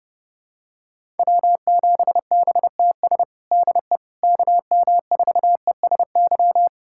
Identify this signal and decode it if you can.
Morse code, heard as a keyed tone, about 30 wpm, 700 Hz. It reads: W7BTS DE KM4ESY
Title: USA Callsign DE Callsign 30wpm